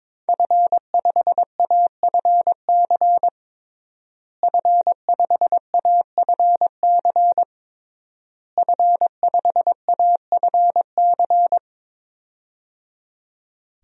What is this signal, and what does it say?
Morse code, heard as a keyed tone, about 22 wpm, 700 Hz. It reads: F5AFC F5AFC F5AFC
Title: DXCC-100 Call Signs - 5 Characters - Repeated 3x - 22wpm